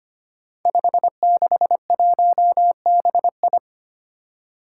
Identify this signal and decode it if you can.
Morse code, heard as a keyed tone, about 25 wpm, 700 Hz. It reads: 561BI
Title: Five Challenging Letter-Number Combinations 25wpm